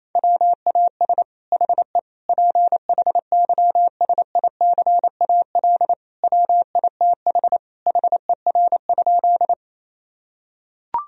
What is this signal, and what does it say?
Morse code, heard as a keyed tone, about 28 wpm, 700 Hz. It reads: WAS HE PHYSICAL WITH HER?
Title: Sentences from Top 1000 Words - Review Part 4 - 28wpm